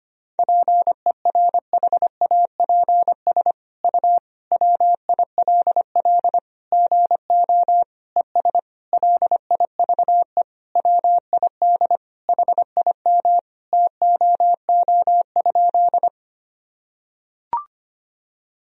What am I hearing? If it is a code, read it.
Morse code, heard as a keyed tone, about 25 wpm, 700 Hz. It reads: PERHAPS U WILL GO ES LIVE WID HIM TOO?
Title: Sentences from Top 500 Words and Abbreviations 25wpm